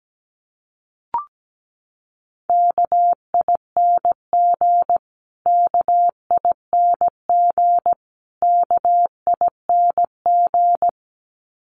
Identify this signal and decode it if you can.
Morse code, heard as a keyed tone, about 17 wpm, 700 Hz. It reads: KING KING KING
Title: Top 500 Words Repeated 17wpm